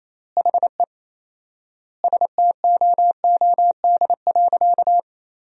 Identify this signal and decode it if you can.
Morse code, heard as a keyed tone, about 28 wpm, 700 Hz. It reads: HE STOOD.
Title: 4x Word-Spacing - Sentences from Top 500 Words - 28wpm